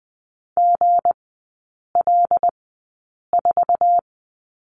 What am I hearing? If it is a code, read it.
Morse code, heard as a keyed tone, about 20 wpm, spaced as for 8 wpm, 700 Hz. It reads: GL4
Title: 178 Lesson 29e - Slash - Three Characters 20wpm